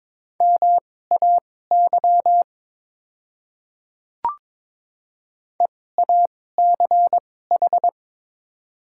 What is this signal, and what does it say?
Morse code, heard as a keyed tone, about 22 wpm, spaced as for 16 wpm, 690 Hz. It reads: MAY EACH
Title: 2x Character Spacing - Top 100 Words - 22wpm